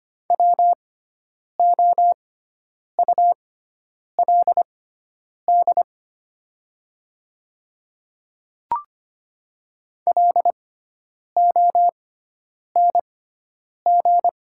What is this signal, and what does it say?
Morse code, heard as a keyed tone, about 25 wpm, spaced as for 9 wpm, 690 Hz. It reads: WOULD LONG
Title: 6x Character Spacing - Top 200 Words - 25wpm